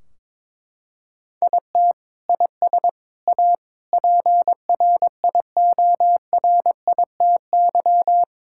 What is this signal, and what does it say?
Morse code, heard as a keyed tone, about 22 wpm, 700 Hz. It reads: IT IS A PRIORITY
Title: Sentences from Top 1600 Words 22wpm